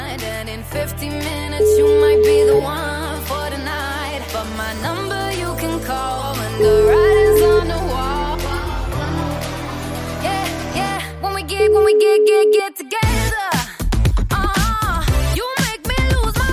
Ringback tone (Europe, 425 Hz), with noise.